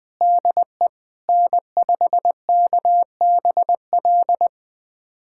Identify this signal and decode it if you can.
Morse code, heard as a keyed tone, about 20 wpm, 700 Hz. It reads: DE N5KBL
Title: USA Callsign DE Callsign - 5 Characters - 20wpm